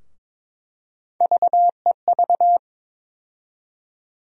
Morse code, heard as a keyed tone, about 22 wpm, 700 Hz. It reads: VEV
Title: USA Call Sign Postfixes - 3 Letters 22wpm